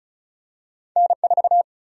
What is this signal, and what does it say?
Morse code, heard as a keyed tone, about 34 wpm, 700 Hz. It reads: N4
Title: USA Call Sign Prefixes - 35wpm